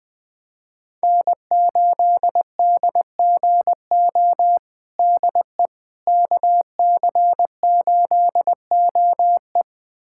Morse code, heard as a keyed tone, about 20 wpm, 700 Hz. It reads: N8DGO DE KC8OE
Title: USA Callsign DE Callsign - 5 Characters - 20wpm